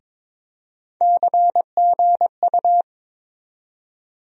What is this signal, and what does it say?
Morse code, heard as a keyed tone, about 22 wpm, 700 Hz. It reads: CGU
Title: USA Call Sign Postfixes - 3 Letters 22wpm